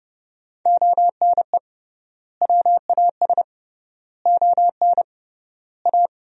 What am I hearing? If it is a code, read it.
Morse code, heard as a keyed tone, about 30 wpm, 700 Hz. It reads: ONE WAS ON A
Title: 3x Word-Spacing - Sentences from Top 300 Words - 30wpm